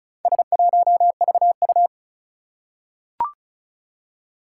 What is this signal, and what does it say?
Morse code, heard as a keyed tone, about 36 wpm, 700 Hz. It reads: S1VU
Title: Four Challenging Letter-Number Combinations 35wpm